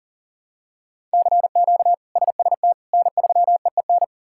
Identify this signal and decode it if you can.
Morse code, heard as a keyed tone, about 40 wpm, 700 Hz. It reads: CQ SST N3EEN
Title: K1USN SST QSO Practice - 40wpm